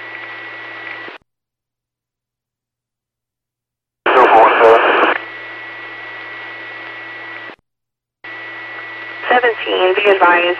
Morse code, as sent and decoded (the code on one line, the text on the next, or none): none